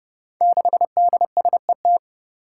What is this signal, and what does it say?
Morse code, heard as a keyed tone, about 30 wpm, 700 Hz. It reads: 6DSET
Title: Five Challenging Letter-Number Combinations 30wpm